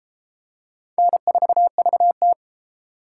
Morse code, heard as a keyed tone, about 28 wpm, 700 Hz. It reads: N4VT